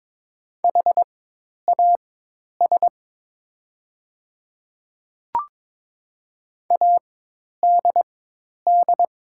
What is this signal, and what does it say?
Morse code, heard as a keyed tone, about 22 wpm, spaced as for 10 wpm, 700 Hz. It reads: HAS ADD
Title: 4x Character Spacing - Top 200 Words - 22wpm